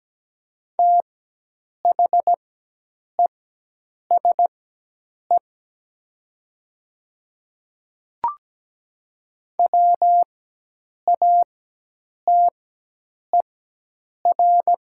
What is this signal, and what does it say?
Morse code, heard as a keyed tone, about 17 wpm, spaced as for 8 wpm, 700 Hz. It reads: THESE WATER
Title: 4x Character Spacing - Top 200 Words - 17wpm